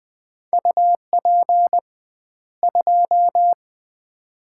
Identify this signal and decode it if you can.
Morse code, heard as a keyed tone, about 20 wpm, 700 Hz. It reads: UP 2
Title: Common QSO Phrases - 2x Word-Spacing - 20wpm